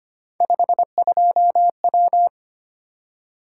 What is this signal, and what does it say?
Morse code, heard as a keyed tone, about 25 wpm, 700 Hz. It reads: 52W
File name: Three Letter-Number Combinations - 25wpm